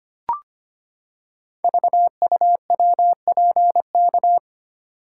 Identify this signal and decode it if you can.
Morse code, heard as a keyed tone, about 25 wpm, 700 Hz. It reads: VUWPK
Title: Code Groups - 5 Letters - 25wpm